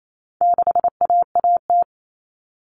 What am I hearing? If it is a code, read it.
Morse code, heard as a keyed tone, about 28 wpm, 700 Hz. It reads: 6AAT
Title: Common Cut Numbers - 4-Digit 28wpm